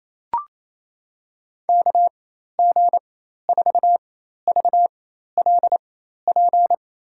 Morse code, heard as a keyed tone, about 28 wpm, spaced as for 13 wpm, 700 Hz. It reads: KG4VLP
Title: USA Call Signs - 6 Characters - 4x Character Spacing - 28wpm